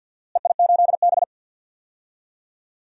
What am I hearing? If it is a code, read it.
Morse code, heard as a keyed tone, about 51 wpm, 700 Hz. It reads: EI8B
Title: DXCC-100 Call Signs - 4 Characters - Repeated 3x - 50wpm